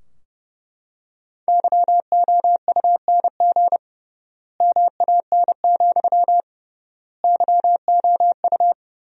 Morse code, heard as a keyed tone, about 30 wpm, 700 Hz. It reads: YOUNG MAN, YOU
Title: 3x Word-Spacing - Sentences from Top 400 Words - 30wpm